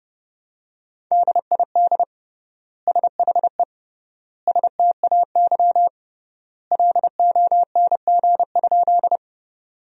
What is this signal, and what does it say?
Morse code, heard as a keyed tone, about 30 wpm, 700 Hz. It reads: DID SHE STAY LONG?
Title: 3x Word-Spacing - Sentences from Top 500 Words - 30wpm